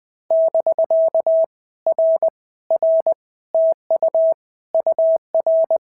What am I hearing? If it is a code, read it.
Morse code, heard as a keyed tone, about 19 wpm, 650 Hz. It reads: <BK> R R TU UR
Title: POTA QSO Practice - 20wpm